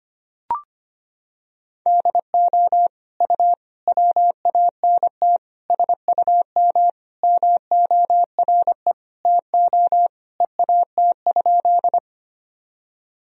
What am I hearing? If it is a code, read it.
Morse code, heard as a keyed tone, about 25 wpm, 700 Hz. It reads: DO U WANT SUM MORE TO EAT?
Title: Sentences from Top 300 Words and Abbreviations 25wpm